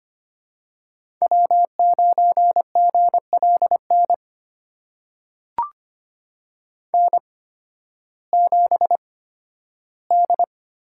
Morse code, heard as a keyed tone, about 25 wpm, spaced as for 13 wpm, 700 Hz. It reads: W9GLN N7D
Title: USA Call Signs - 5 Characters - 8x Character Spacing - 25wpm